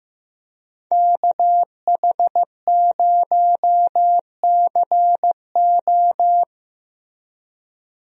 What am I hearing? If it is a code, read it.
Morse code, heard as a keyed tone, about 15 wpm, 700 Hz. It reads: KH0CO
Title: USA Call Signs - 5 Characters - Repeated 2x - 15wpm